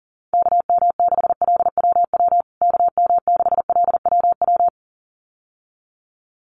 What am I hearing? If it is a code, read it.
Morse code, heard as a keyed tone, about 40 wpm, 700 Hz. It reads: KM6LWW KM6LWW
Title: USA Call Signs Repeated 2x 40wpm